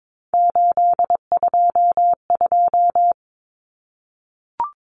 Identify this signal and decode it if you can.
Morse code, heard as a keyed tone, about 22 wpm, 700 Hz. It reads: 822
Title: Numbers - 3-Digit 22wpm